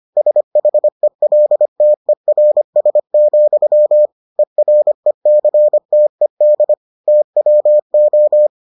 Morse code, heard as a keyed tone, about 25 wpm, 600 Hz. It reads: SHELTERS, ERECTED TWO